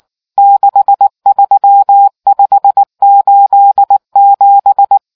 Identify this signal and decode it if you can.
Morse code, heard as a keyed tone, about 19 wpm, 800 Hz. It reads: 63587